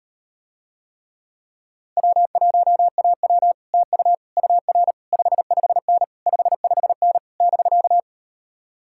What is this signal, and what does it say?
Morse code, heard as a keyed tone, about 38 wpm, 700 Hz. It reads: W1AW TU UR 55N 55N <BK>